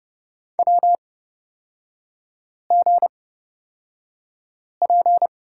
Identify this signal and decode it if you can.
Morse code, heard as a keyed tone, about 30 wpm, spaced as for 5 wpm, 700 Hz. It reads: WGP
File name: Lesson 36e - BT - Three Characters 30wpm